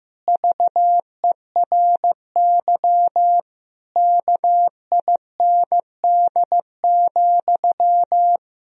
Morse code, heard as a keyed tone, about 15 wpm, 700 Hz. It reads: VERY KIND,
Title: Sentences from Top 1000 Words - Review Part 1 - 15wpm